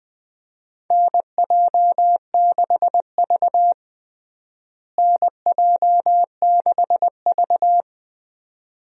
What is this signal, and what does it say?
Morse code, heard as a keyed tone, about 20 wpm, 700 Hz. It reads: NJ6V NJ6V
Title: USA Call Signs - 4 Characters - Repeated 2x - 20wpm